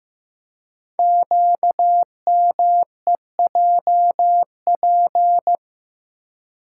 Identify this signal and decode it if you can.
Morse code, heard as a keyed tone, about 15 wpm, 700 Hz. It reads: QMEJP
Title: Code Groups - 5 Letter:Numbers - 15wpm